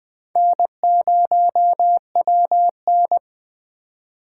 Morse code, heard as a keyed tone, about 20 wpm, 700 Hz. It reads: N0WN
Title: USA Call Signs - 4 Characters - 3x Character Spacing - 20wpm